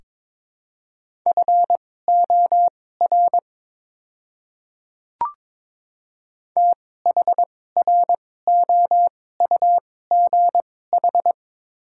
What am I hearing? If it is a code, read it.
Morse code, heard as a keyed tone, about 22 wpm, spaced as for 16 wpm, 700 Hz. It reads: FOR THROUGH
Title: 2x Character Spacing - Top 200 Words - 22wpm